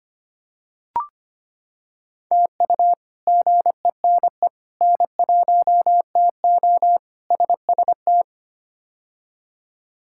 Morse code, heard as a keyed tone, about 25 wpm, 700 Hz. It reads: TU GENE N1TO SST
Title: K1USN SST QSO Practice - 25wpm